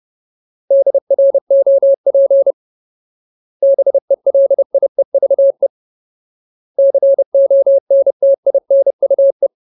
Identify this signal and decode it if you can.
Morse code, heard as a keyed tone, about 30 wpm, 550 Hz. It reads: DROP BELIEVE CONTINUE
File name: Ultimate Word List / 30WPM Ultimate